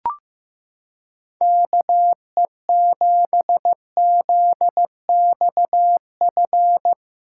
Morse code, heard as a keyed tone, about 15 wpm, 700 Hz. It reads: KE7ZXF
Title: USA Callsign DE Callsign - 6 Characters - 15wpm